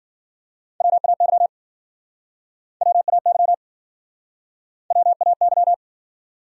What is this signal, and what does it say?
Morse code, heard as a keyed tone, about 56 wpm, 700 Hz. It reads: WAY WAY WAY